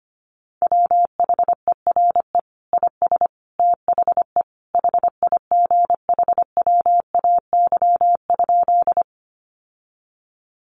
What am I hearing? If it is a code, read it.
Morse code, heard as a keyed tone, about 25 wpm, 700 Hz. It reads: WHERE IS THE HIGHWAY?
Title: Sentences from Top 1400 Words 25wpm